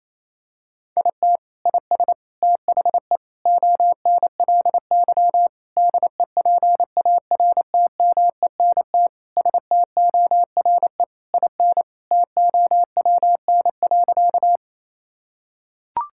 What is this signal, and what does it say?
Morse code, heard as a keyed tone, about 28 wpm, 700 Hz. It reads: IT IS THE ONLY DEPARTMENT STORE IN TOWN.
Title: Sentences from Top 1000 Words - Review Part 5 - 28wpm